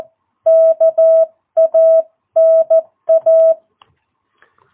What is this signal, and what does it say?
Morse code, heard as a keyed tone, about 14 wpm, 660 Hz. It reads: KANA
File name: jh8cbh / 2022cw